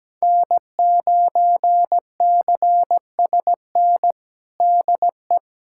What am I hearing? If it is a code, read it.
Morse code, heard as a keyed tone, about 17 wpm, 700 Hz. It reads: N9CSN DE
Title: USA Callsign DE Callsign 17wpm